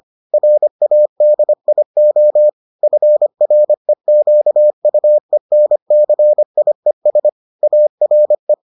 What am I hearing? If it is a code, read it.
Morse code, heard as a keyed tone, about 25 wpm, 600 Hz. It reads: RADIO FREQUENCIES ARE